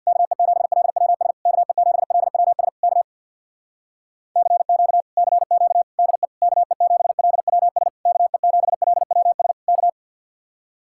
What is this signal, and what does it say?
Morse code, heard as a keyed tone, about 65 wpm, 700 Hz. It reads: KE7LWS KE7LWS K CQ CQ DE KE7LWS KE7LWS K